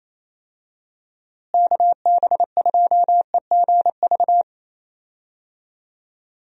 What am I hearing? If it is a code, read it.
Morse code, heard as a keyed tone, about 28 wpm, 700 Hz. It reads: KB2EGV
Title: USA Call Signs - 6 Characters - Repeated 3x - 28wpm